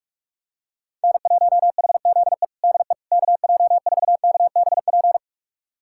Morse code, heard as a keyed tone, about 46 wpm, 700 Hz. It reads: N1HZE DE KJ4KBP
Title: USA Callsign DE Callsign 45wpm